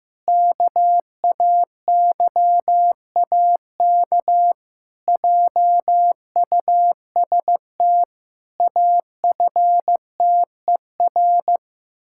Morse code, heard as a keyed tone, about 15 wpm, 700 Hz. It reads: KAYAK JUST AFTER
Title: Pangram Challenge v2 - 15wpm